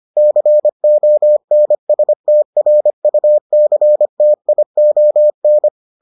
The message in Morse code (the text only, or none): CONSTRUCTION